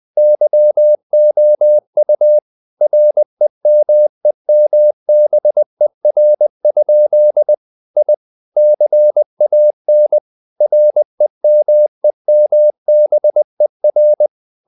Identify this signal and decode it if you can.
Morse code, heard as a keyed tone, about 20 wpm, 600 Hz. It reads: YOU REMEMBER? I CAN REMEMBER
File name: the old man and the sea / the old man and the sea part 1 20wpm